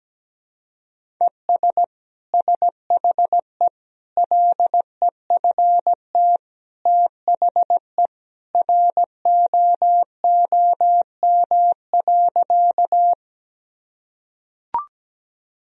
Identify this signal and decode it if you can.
Morse code, heard as a keyed tone, about 17 wpm, 700 Hz. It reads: ES SHE LEFT THE ROOM.
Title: Sentences from Top 300 Words and Abbreviations 17wpm